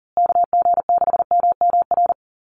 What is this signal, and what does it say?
Morse code, heard as a keyed tone, about 40 wpm, 700 Hz. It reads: KG6MMR
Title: USA Call Signs Repeated 3x 40wpm